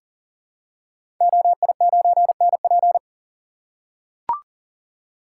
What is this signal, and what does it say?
Morse code, heard as a keyed tone, about 40 wpm, 700 Hz. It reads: OI9NP
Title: Code Groups - 5 Letter:Numbers - 40wpm